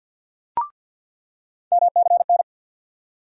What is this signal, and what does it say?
Morse code, heard as a keyed tone, about 49 wpm, 700 Hz. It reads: MCN
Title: USA Call Sign Postfixes - 3 Letters 50wpm